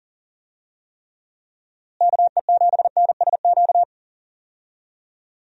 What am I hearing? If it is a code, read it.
Morse code, heard as a keyed tone, about 40 wpm, 700 Hz. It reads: KE7NSQ
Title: USA Call Signs - 6 Characters - Repeated 3x - 40wpm